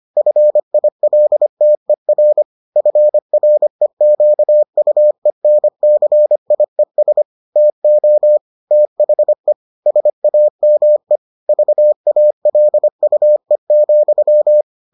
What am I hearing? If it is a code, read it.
Morse code, heard as a keyed tone, about 25 wpm, 600 Hz. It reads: FILTER FREQUENCIES TO THE SAME VALUE,